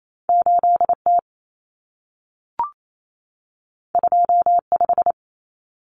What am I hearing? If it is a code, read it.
Morse code, heard as a keyed tone, about 28 wpm, 700 Hz. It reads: 8T 25
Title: Common Cut Numbers - 2-Digit 28wpm